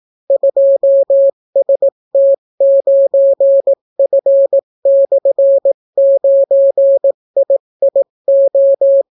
Morse code, heard as a keyed tone, about 18 wpm, 550 Hz. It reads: 2ST9F/9IIO